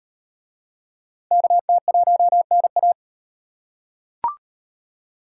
Morse code, heard as a keyed tone, about 38 wpm, 700 Hz. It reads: KT1NA